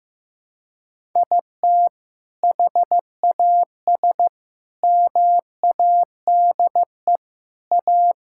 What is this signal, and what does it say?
Morse code, heard as a keyed tone, about 15 wpm, 700 Hz. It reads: IT HAS MADE A
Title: Sentences from Top 1000 Words - Review Part 5 - 15wpm